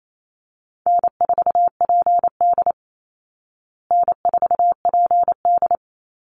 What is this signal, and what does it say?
Morse code, heard as a keyed tone, about 28 wpm, 700 Hz. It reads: N4PD N4PD
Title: Call Signs Repeated 2x - 28wpm